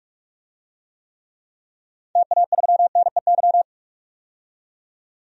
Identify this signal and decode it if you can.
Morse code, heard as a keyed tone, about 46 wpm, 700 Hz. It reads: TA3NEY
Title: DXCC-100 Call Signs - 6 Characters - 45wpm